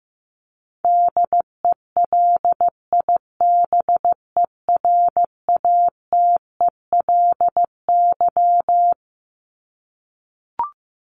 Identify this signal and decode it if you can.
Morse code, heard as a keyed tone, about 15 wpm, 700 Hz. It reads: DELIBERATELY
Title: Top 1751-2000 Words 15wpm